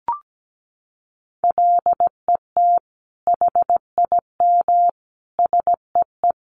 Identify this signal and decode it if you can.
Morse code, heard as a keyed tone, about 17 wpm, 700 Hz. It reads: LET HIM SEE
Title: Sentences from Top 1400 Words 17wpm